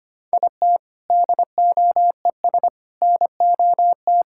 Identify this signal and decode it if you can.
Morse code, heard as a keyed tone, about 25 wpm, 700 Hz. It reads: IT DOES NOT
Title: Sentences from Top 1000 Words - Review Part 6 - 25wpm